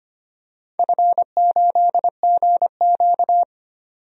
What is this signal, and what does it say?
Morse code, heard as a keyed tone, about 25 wpm, 700 Hz. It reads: F8GQ
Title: DXCC-100 Call Signs - 4 Characters - 25wpm